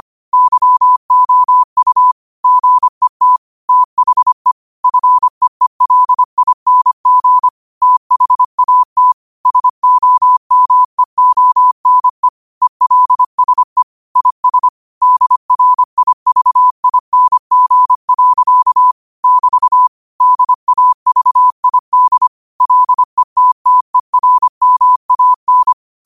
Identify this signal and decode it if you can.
Morse code, heard as a keyed tone, about 25 wpm, 1000 Hz. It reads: YOU GET THE FEELING THAT SOMEONE ELSE IS DRIVING. <BT> DAVID LETTERMAN
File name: qotdfiles / 25wpm